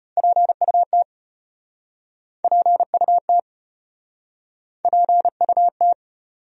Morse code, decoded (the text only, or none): PUT PUT PUT